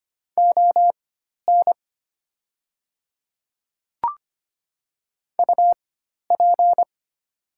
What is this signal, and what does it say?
Morse code, heard as a keyed tone, about 25 wpm, spaced as for 12 wpm, 690 Hz. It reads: ON UP
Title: 4x Character Spacing - Top 100 Words - 25wpm